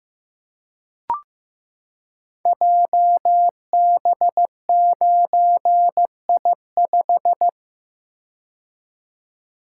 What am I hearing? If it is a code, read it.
Morse code, heard as a keyed tone, about 15 wpm, 700 Hz. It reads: JB9I5
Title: Code Groups - 5 Letter:Numbers - 15wpm